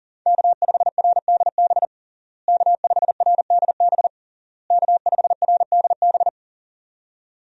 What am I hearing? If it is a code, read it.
Morse code, heard as a keyed tone, about 40 wpm, 700 Hz. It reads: K5RDB K5RDB K5RDB
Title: USA Call Signs - 5 Characters - Repeated 3x - 40wpm